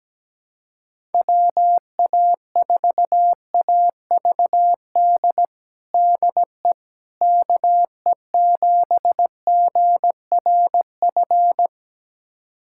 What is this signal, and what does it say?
Morse code, heard as a keyed tone, about 17 wpm, 700 Hz. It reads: WA4AVD DE KE7GRF
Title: USA Callsign DE Callsign 17wpm